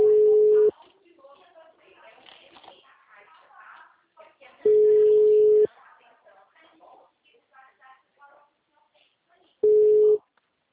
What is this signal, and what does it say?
Ringback tone (Europe, 425 Hz), with speech.